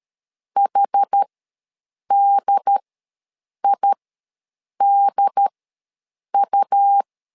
Morse code, heard as a keyed tone, about 13 wpm, spaced as for 7 wpm, 800 Hz. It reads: HDIDU
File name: LEZIONI / 2 UHDN / 2 UHD ws 12wpm 1722